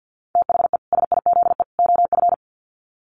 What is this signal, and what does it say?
Morse code, heard as a keyed tone, about 50 wpm, 700 Hz. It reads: THE SIZE OF
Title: Sets of 3 Words 50wpm